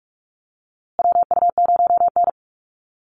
Morse code, heard as a keyed tone, about 44 wpm, 700 Hz. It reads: WU0N
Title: USA Call Signs Repeated 2x 45wpm